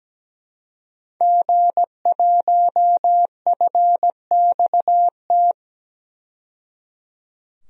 Morse code, heard as a keyed tone, about 17 wpm, 700 Hz. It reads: G1FXT
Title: DXCC-100 Call Signs - 5 Characters - 17wpm